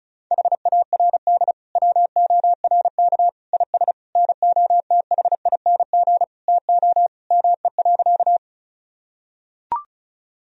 Morse code, heard as a keyed tone, about 34 wpm, 700 Hz. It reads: HARD WORK IS NOTHING TO ME.
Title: Sentences from Top 1000 Words - Review Part 1 - 35wpm